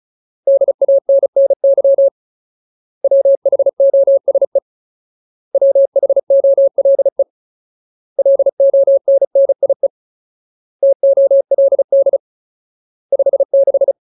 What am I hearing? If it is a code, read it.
Morse code, heard as a keyed tone, about 34 wpm, 550 Hz. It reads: DANNY WHOSE WHOLE LONNIE TOLD 56
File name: Ultimate Word List / 35WPM Ultimate